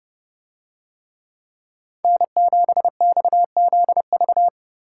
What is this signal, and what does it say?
Morse code, heard as a keyed tone, about 30 wpm, 700 Hz. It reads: N7XZV